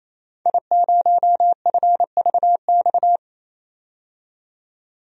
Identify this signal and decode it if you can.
Morse code, heard as a keyed tone, about 28 wpm, 700 Hz. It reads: I0FVX
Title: DXCC-100 Call Signs - 5 Characters - 28wpm